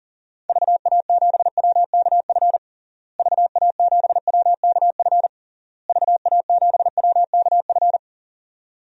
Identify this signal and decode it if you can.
Morse code, heard as a keyed tone, about 40 wpm, 700 Hz. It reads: VA7WKF VA7WKF VA7WKF
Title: DXCC-100 Call Signs - 6 Characters - Repeated 3x - 40wpm